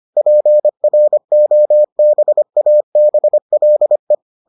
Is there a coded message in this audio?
Yes: Morse code from 0.2 to 4.1 s, heard as a keyed tone, about 25 wpm, 600 Hz. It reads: PROBABLE